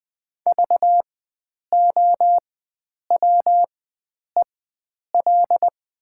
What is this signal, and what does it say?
Morse code, heard as a keyed tone, about 21 wpm, spaced as for 9 wpm, 700 Hz. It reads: VOWEL
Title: 4x Character Spacing - Top 500 Words - 20wpm